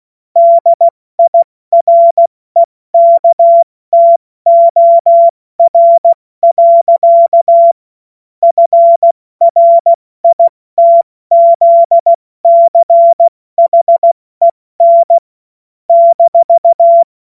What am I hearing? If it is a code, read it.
Morse code, heard as a keyed tone, about 16 wpm, spaced as for 14 wpm, 680 Hz. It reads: DIREKTOR. FRITZCHEN -
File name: Morsekurs / Witze / Witz 10. August 2021, 70 F80